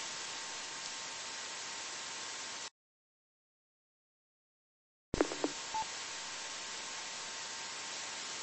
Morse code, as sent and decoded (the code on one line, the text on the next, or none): none